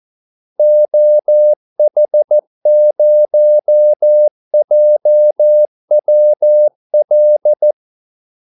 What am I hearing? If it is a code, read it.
Morse code, heard as a keyed tone, about 14 wpm, 600 Hz. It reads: OH0JWL
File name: Callsigns-14wpm